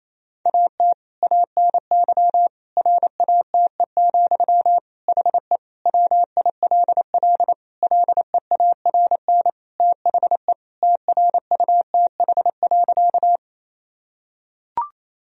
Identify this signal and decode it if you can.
Morse code, heard as a keyed tone, about 28 wpm, 700 Hz. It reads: AT ANY RATE, HE WILL LEARN THE TRUTH.
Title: Sentences from Top 1000 Words - Review Part 4 - 28wpm